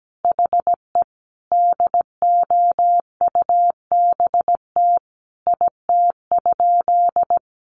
Morse code, heard as a keyed tone, about 17 wpm, 700 Hz. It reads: HE DOUBT IT?
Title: Sentences from Top 1500 Words 17wpm